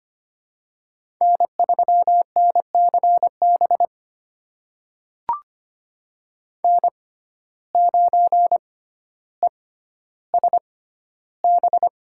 Morse code, heard as a keyed tone, about 25 wpm, spaced as for 13 wpm, 700 Hz. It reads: N3NCB N9ESB
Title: USA Call Signs - 5 Characters - 6x Character Spacing - 25wpm